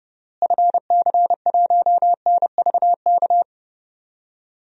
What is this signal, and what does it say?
Morse code, heard as a keyed tone, about 32 wpm, 700 Hz. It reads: FC1NVK
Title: DXCC-100 Call Signs - 6 Characters - Repeated 3x - 30wpm